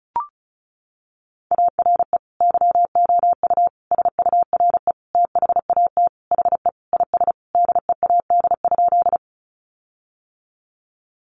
Morse code, heard as a keyed tone, about 34 wpm, 700 Hz. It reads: ARE YOU SURE THAT HE IS DEAD?
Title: Sentences from Top 1500 Words 35wpm